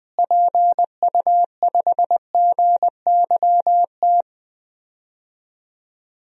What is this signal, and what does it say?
Morse code, heard as a keyed tone, about 21 wpm, 700 Hz. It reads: PU5GYT
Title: DXCC-100 Call Signs - 6 Characters - Repeated 2x - 20wpm